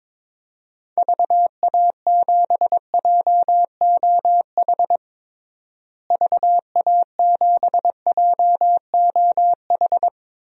Morse code, heard as a keyed tone, about 22 wpm, 700 Hz. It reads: VA7JOH VA7JOH
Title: DXCC-100 Call Signs - 6 Characters - Repeated 2x - 22wpm